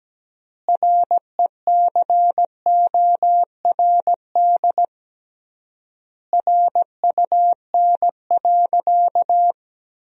Morse code, heard as a keyed tone, about 17 wpm, 700 Hz. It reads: RECORD RUN.
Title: 3x Word-Spacing - Sentences from Top 500 Words - 17wpm